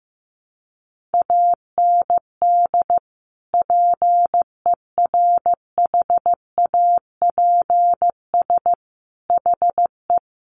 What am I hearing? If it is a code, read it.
Morse code, heard as a keyed tone, about 15 wpm, 700 Hz. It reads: AND PERHAPS HE
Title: Sentences from Top 900 Words 15wpm